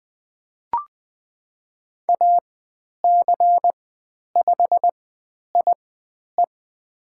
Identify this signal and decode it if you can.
Morse code, heard as a keyed tone, about 20 wpm, spaced as for 10 wpm, 700 Hz. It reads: AC5IE
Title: 102 Lesson 17g - Letter u - Call Signs 20wpm